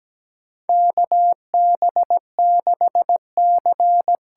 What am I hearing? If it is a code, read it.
Morse code, heard as a keyed tone, about 17 wpm, 700 Hz. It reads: KB6C
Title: USA Call Signs - 6 Characters - 8x Character Spacing - 17wpm